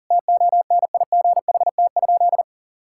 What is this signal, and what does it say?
Morse code, heard as a keyed tone, about 37 wpm, 700 Hz. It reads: TONIGHT?
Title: Sentences from Top 1000 Words - Review Part 5 - 40wpm